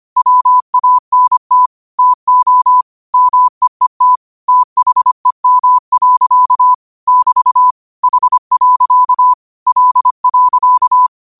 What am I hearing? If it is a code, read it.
Morse code, heard as a keyed tone, about 25 wpm, 1000 Hz. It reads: WANT TO MEET THEM. <BT> H. L.
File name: qotdfiles / 25wpm